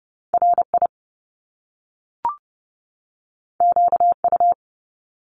Morse code, heard as a keyed tone, about 30 wpm, 700 Hz. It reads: RI QU
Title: Top Two Letter Combinations in Words 30wpm